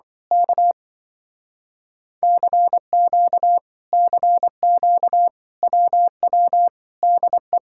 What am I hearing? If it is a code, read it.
Morse code, heard as a keyed tone, about 24 wpm, 700 Hz. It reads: K CQ CQ WW DE
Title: Speed-Racing Common QSO Phrases - 20wpm